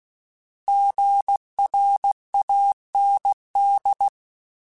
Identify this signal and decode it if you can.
Morse code, heard as a keyed tone, about 16 wpm, 800 Hz. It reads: GRAND
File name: Alphabetical / Alphabetical mp3s / Gs @15